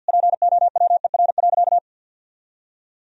Morse code, heard as a keyed tone, about 50 wpm, 700 Hz. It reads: POWER.
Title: Sentences from Top 1000 Words - Review Part 3 - 50wpm